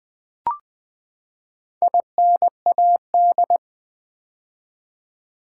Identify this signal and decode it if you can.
Morse code, heard as a keyed tone, about 20 wpm, 700 Hz. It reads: INAD
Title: Build A Word - 4 Letters At A Time - 10-Letter Words - 20wpm